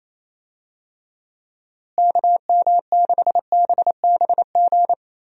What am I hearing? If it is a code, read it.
Morse code, heard as a keyed tone, about 27 wpm, 700 Hz. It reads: KM6BBG
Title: USA Call Signs - 6 Characters - Repeated 2x - 28wpm